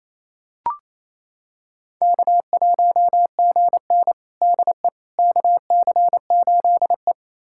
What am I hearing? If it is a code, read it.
Morse code, heard as a keyed tone, about 28 wpm, 700 Hz. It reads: K1GN DE KC8E
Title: USA Callsign DE Callsign - 4 Characters - 28wpm